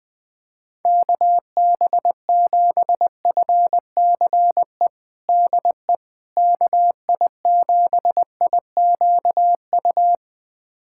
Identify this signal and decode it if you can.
Morse code, heard as a keyed tone, about 20 wpm, 700 Hz. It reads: KB7FCE DE KI7IQU
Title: USA Callsign DE Callsign - 6 Characters - 20wpm